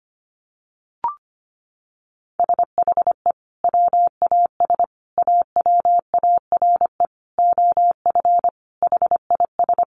Morse code, heard as a keyed tone, about 25 wpm, 700 Hz. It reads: SHE WAS AWARE OF HIS
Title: Sentences from Top 1200 Words 25wpm